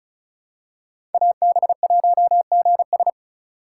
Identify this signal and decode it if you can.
Morse code, heard as a keyed tone, about 36 wpm, 700 Hz. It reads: AB1GS
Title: USA Call Signs - 5 Characters - 2x Character Spacing - 35wpm